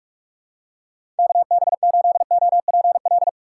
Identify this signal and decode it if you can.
Morse code, heard as a keyed tone, about 46 wpm, 700 Hz. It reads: KB8OPL
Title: USA Call Signs - 6 Characters - 2x Character Spacing - 45wpm